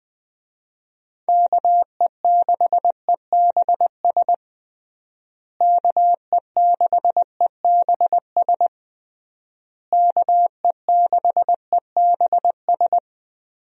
Morse code, heard as a keyed tone, about 20 wpm, 700 Hz. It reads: KE6EBS KE6EBS KE6EBS
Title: USA Call Signs - 6 Characters - Repeated 3x - 20wpm